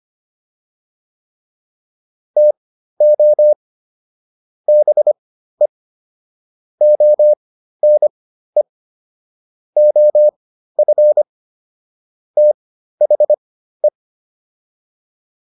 Morse code, heard as a keyed tone, about 25 wpm, spaced as for 13 wpm, 600 Hz. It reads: TO BE ONE OF THE